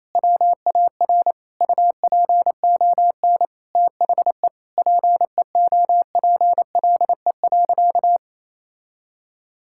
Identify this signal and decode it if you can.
Morse code, heard as a keyed tone, about 28 wpm, 700 Hz. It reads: WAR UPON THE PEOPLE.
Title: Sentences from Top 1000 Words - Review Part 6 - 28wpm